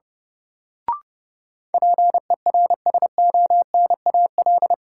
Morse code, heard as a keyed tone, about 29 wpm, 700 Hz. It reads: PERSONAL
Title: Sentences from Top 700 Words - New Word Drill - 30wpm